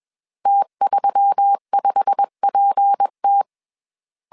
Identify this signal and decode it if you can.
Morse code, heard as a keyed tone, about 21 wpm, 800 Hz. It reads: T35PT
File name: CW / DET / 1 9 1500C S B 20wpm 2313